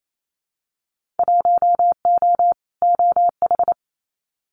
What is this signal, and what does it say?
Morse code, heard as a keyed tone, about 28 wpm, 700 Hz. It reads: 1O OH
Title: Winter Field Day Exchanges 28wpm